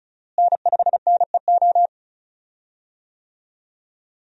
Morse code, heard as a keyed tone, about 34 wpm, 700 Hz. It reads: N5NEO
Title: USA Call Signs - 5 Characters - Repeated 2x - 35wpm